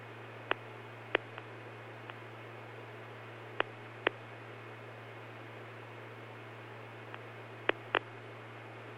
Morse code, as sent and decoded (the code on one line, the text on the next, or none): none